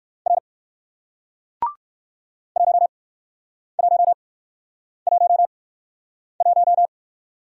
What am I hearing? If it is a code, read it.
Morse code, heard as a keyed tone, about 60 wpm, spaced as for 9 wpm, 700 Hz. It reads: R 1111